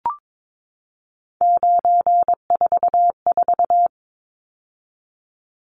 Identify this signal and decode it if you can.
Morse code, heard as a keyed tone, about 22 wpm, 700 Hz. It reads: 944
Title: Numbers - 3-Digit 22wpm